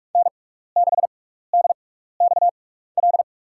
Morse code, heard as a keyed tone, about 47 wpm, spaced as for 16 wpm, 700 Hz. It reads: N6DXL
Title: USA Call Signs - 6x Character Spacing - 45wpm